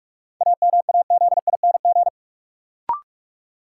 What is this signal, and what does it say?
Morse code, heard as a keyed tone, about 46 wpm, 700 Hz. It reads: AMAZING
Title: Sentences from Top 1500 Words - New Word Drill - 45wpm